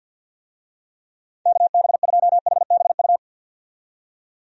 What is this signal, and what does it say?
Morse code, heard as a keyed tone, about 48 wpm, 700 Hz. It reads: KB2HBU
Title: USA Call Signs - 6 Characters - 50wpm